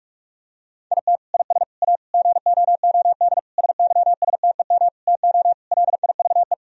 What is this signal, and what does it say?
Morse code, heard as a keyed tone, about 46 wpm, 700 Hz. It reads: IT IS A GOOD SYSTEM TO LIVE
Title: Sentences from Top 1000 Words - Review Part 3 - 45wpm